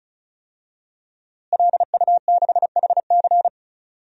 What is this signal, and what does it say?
Morse code, heard as a keyed tone, about 36 wpm, 700 Hz. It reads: LU6HC
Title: DXCC-100 Call Signs - 5 Characters - Repeated 2x - 35wpm